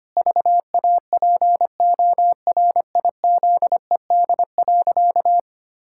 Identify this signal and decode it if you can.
Morse code, heard as a keyed tone, about 25 wpm, 700 Hz. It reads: VAPORIZED.
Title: Pangram Challenge v2 - 25wpm